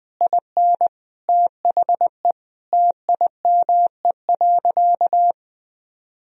Morse code, heard as a keyed tone, about 21 wpm, 700 Hz. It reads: IN THE TIME.
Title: Sentences from Top 200 Words and Abbreviations 20wpm